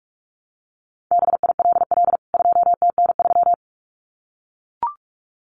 Morse code, heard as a keyed tone, about 44 wpm, 700 Hz. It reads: BILL 2TN3
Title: CWT Exchange 45wpm